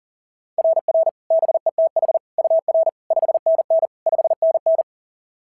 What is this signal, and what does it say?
Morse code, heard as a keyed tone, about 37 wpm, 690 Hz. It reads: RR BETH UR 5NN 5NN
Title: SOTA QSO Practice 40wpm